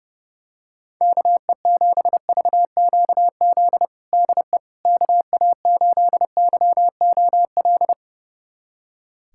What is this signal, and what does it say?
Morse code, heard as a keyed tone, about 32 wpm, 700 Hz. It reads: KE7VQZ DE KA8YOL
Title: USA Callsign DE Callsign - 6 Characters - 30wpm